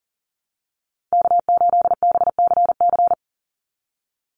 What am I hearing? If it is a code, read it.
Morse code, heard as a keyed tone, about 40 wpm, 700 Hz. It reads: K8BCC
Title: USA Call Signs 40wpm